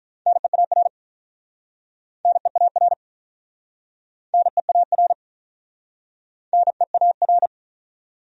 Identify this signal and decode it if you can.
Morse code, heard as a keyed tone, about 48 wpm, 700 Hz. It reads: NEAR NEAR NEAR NEAR